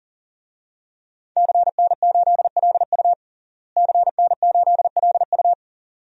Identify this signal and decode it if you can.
Morse code, heard as a keyed tone, about 37 wpm, 700 Hz. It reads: CN8LU CN8LU
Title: DXCC-100 Call Signs - 5 Characters - Repeated 2x - 40wpm